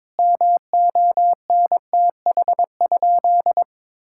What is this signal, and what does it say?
Morse code, heard as a keyed tone, about 22 wpm, 700 Hz. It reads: MONTH?
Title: Sentences from Top 1000 Words - Review Part 6 - 22wpm